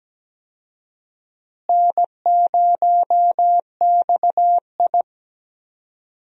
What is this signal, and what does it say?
Morse code, heard as a keyed tone, about 17 wpm, 700 Hz. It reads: N0XI
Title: USA Call Signs - 4 Characters - Repeated 3x - 17wpm